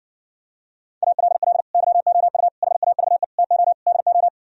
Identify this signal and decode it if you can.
Morse code, heard as a keyed tone, about 59 wpm, 700 Hz. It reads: ALL YOU HAVE TO DO